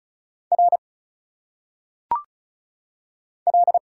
Morse code, heard as a keyed tone, about 34 wpm, 700 Hz. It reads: R L
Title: Single Letters - Phonetic Pronunciation 35wpm